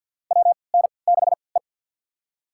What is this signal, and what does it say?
Morse code, heard as a keyed tone, about 53 wpm, spaced as for 29 wpm, 700 Hz. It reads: WN6E
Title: USA Call Signs - 4 Characters - 3x Character Spacing - 50wpm